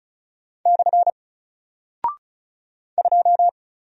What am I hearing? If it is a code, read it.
Morse code, heard as a keyed tone, about 36 wpm, 700 Hz. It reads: / 2
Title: Single Character (Letter:Number:Punctuation:Prosign) 35wpm